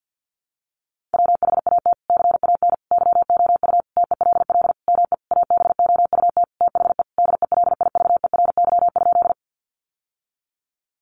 Morse code, heard as a keyed tone, about 50 wpm, 700 Hz. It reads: WHAT CAN YOU TELL ME ABOUT THE DELIVERY?
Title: Sentences from Top 1500 Words 50wpm